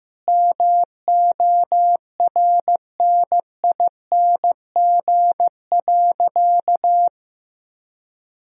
Morse code, heard as a keyed tone, about 15 wpm, 700 Hz. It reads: MORNING.
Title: Sentences from Top 1000 Words - Review Part 1 - 15wpm